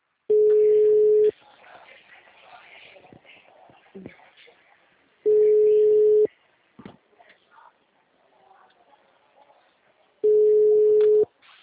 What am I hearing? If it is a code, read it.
Ringback tone (Europe, 425 Hz), with speech.
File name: out- -2019